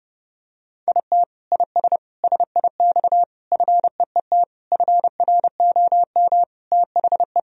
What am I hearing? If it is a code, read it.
Morse code, heard as a keyed tone, about 30 wpm, 700 Hz. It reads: IT IS SIX FEET FROM THE
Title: Sentences from Top 1000 Words - Review Part 2 - 30wpm